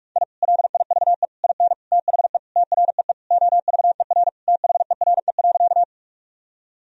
Morse code, heard as a keyed tone, about 46 wpm, 700 Hz. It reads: I LIVE IN THE TREE OVER THERE.
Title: Sentences from Top 1000 Words - Review Part 1 - 45wpm